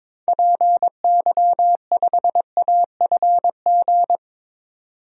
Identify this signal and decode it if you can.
Morse code, heard as a keyed tone, about 22 wpm, 700 Hz. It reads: PY5AFG
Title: DXCC-100 Call Signs - 6 Characters - 22wpm